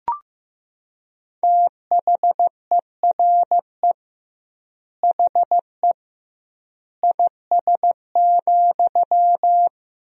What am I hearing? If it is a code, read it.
Morse code, heard as a keyed tone, about 15 wpm, 700 Hz. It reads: THERE HE IS,
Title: 2x Word-Spacing - Sentences from Top 100 Words - 15wpm